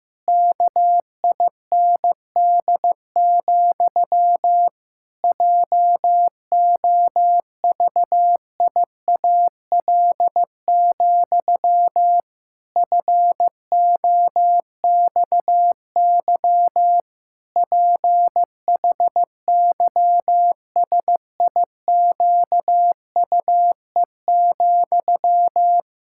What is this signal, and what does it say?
Morse code, heard as a keyed tone, about 15 wpm, 700 Hz. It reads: KIND, JOVIAL, FOXY PHYSIQUE,
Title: Pangram Challenge v2 - 15wpm